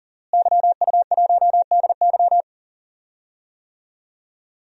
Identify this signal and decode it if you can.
Morse code, heard as a keyed tone, about 37 wpm, 700 Hz. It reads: YU1DY
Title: DXCC-100 Call Signs - 5 Characters - Repeated 3x - 40wpm